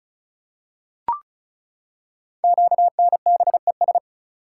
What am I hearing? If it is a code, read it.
Morse code, heard as a keyed tone, about 35 wpm, 700 Hz. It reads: QNBES
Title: Code Groups - 5 Letters - 35wpm